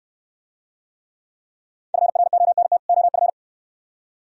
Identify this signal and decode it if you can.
Morse code, heard as a keyed tone, about 69 wpm, 690 Hz. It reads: FRONT OF